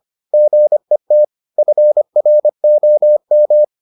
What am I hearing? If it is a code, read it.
Morse code, heard as a keyed tone, about 25 wpm, 600 Hz. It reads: GET FROM